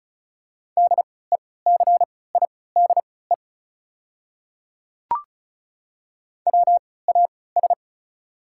Morse code, heard as a keyed tone, about 34 wpm, spaced as for 20 wpm, 690 Hz. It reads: DECIDE WAS
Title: 3x Character Spacing - Top 500 Words - 35wpm